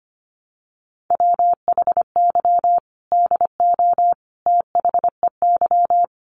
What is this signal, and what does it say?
Morse code, heard as a keyed tone, about 25 wpm, 700 Hz. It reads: WHY DO THEY
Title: Sentences from Top 700 Words 25wpm